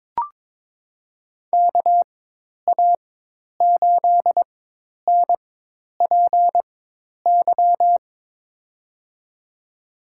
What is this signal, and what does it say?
Morse code, heard as a keyed tone, about 23 wpm, spaced as for 10 wpm, 700 Hz. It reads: KA8NPY
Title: USA Call Signs - 4x Character Spacing - 22wpm